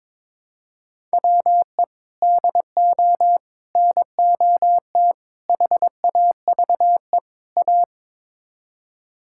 Morse code, heard as a keyed tone, about 22 wpm, 700 Hz. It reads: WE DO NOT HAVE A
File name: practice / 22wpm / Build A Phrase - 5-Word Phrases - 22wpm